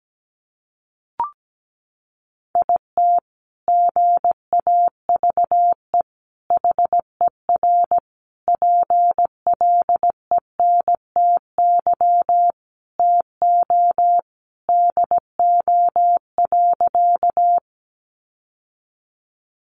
Morse code, heard as a keyed tone, about 17 wpm, 700 Hz. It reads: IT GAVE HER PLENTY TO DO.
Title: Sentences from Top 1300 Words 17wpm